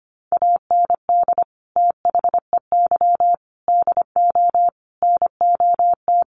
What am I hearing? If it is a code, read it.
Morse code, heard as a keyed tone, about 25 wpm, 700 Hz. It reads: AND THEY DO NOT
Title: Sets of 4 Words 25wpm